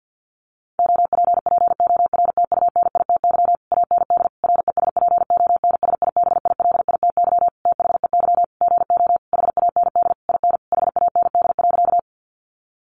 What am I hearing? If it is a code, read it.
Morse code, heard as a keyed tone, about 50 wpm, 700 Hz. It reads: OPPORTUNITY AND RESPONSIBILITY THEY GO HAND IN HAND.
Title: Sentences from Top 1200 Words 50wpm